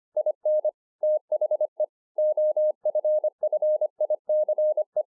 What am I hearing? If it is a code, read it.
Morse code, heard as a keyed tone, about 25 wpm, 610 Hz. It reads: IN THE OFFICE